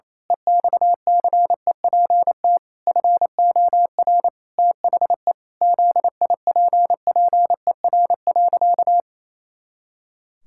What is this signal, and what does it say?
Morse code, heard as a keyed tone, about 28 wpm, 700 Hz. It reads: EXCEPT FOR THE ZIPPER.
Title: Pangram Challenge v2 - 28wpm